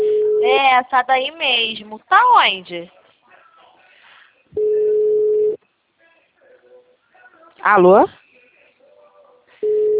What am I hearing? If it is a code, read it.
Ringback tone (Europe, 425 Hz), with speech.
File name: out- -2012- - - .5262